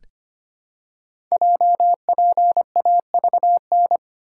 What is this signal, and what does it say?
Morse code, heard as a keyed tone, about 25 wpm, 700 Hz. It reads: JPAVN
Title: Code Groups - 5 Letters - 25wpm